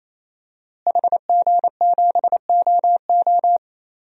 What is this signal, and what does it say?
Morse code, heard as a keyed tone, about 28 wpm, 700 Hz. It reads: HG7OO
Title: DXCC-100 Call Signs - 5 Characters - Repeated 3x - 28wpm